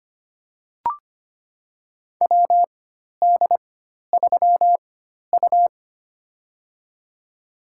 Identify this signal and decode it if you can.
Morse code, heard as a keyed tone, about 25 wpm, spaced as for 12 wpm, 700 Hz. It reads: WD3U
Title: USA Call Signs - 4 Characters - 4x Character Spacing - 25wpm